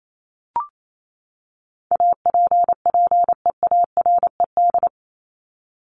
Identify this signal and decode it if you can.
Morse code, heard as a keyed tone, about 28 wpm, 700 Hz. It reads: APPEARED
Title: ED Suffix - Top 100 Words with Base Words 28wpm